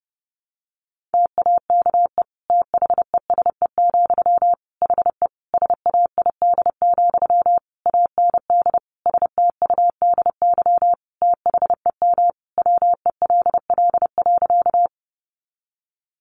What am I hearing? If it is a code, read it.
Morse code, heard as a keyed tone, about 30 wpm, 700 Hz. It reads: TAKE THESE, HE SAID, AND STUDY THEM WELL.
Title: Sentences from Top 300 Words 30wpm